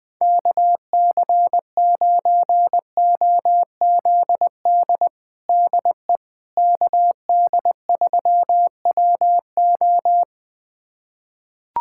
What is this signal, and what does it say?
Morse code, heard as a keyed tone, about 21 wpm, 700 Hz. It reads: KC9OZD DE KD3WO
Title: USA Callsign DE Callsign 20wpm